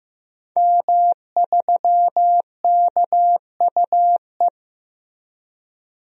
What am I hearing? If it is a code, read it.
Morse code, heard as a keyed tone, about 15 wpm, 700 Hz. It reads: M3KUE
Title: DXCC-100 Call Signs - 5 Characters - Repeated 2x - 15wpm